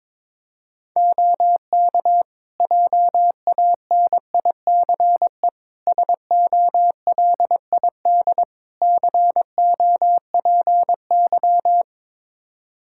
Morse code, heard as a keyed tone, about 22 wpm, 700 Hz. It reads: OK JANICE SOLID COPY
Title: Practice QSO - Exchange 3 - More Info - 22wpm